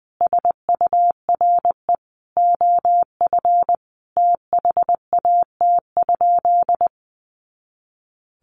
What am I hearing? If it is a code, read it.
Morse code, heard as a keyed tone, about 20 wpm, 700 Hz. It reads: SURE OF THAT?
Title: Sentences from Top 300 Words 20wpm